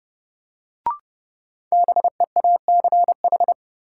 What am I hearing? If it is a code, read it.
Morse code, heard as a keyed tone, about 31 wpm, 700 Hz. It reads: BEACH
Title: Sentences from Top 1500 Words - New Word Drill Alt - 30wpm